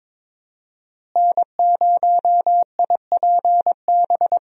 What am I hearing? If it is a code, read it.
Morse code, heard as a keyed tone, about 22 wpm, 700 Hz. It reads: N0IPB
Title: USA Call Signs - 5 Characters - Repeated 2x - 22wpm